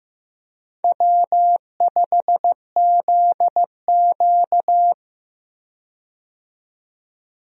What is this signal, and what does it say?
Morse code, heard as a keyed tone, about 15 wpm, 700 Hz. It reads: W5ZQ
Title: USA Call Signs - 4 Characters - 15wpm